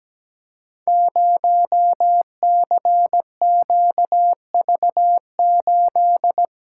Morse code, heard as a keyed tone, about 17 wpm, 700 Hz. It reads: 0CQV8
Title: Code Groups - 5 Letter:Numbers - 17wpm